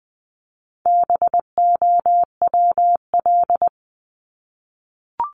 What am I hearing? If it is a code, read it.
Morse code, heard as a keyed tone, about 20 wpm, 700 Hz. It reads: BOWL
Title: Top 1001-1250 Words 20wpm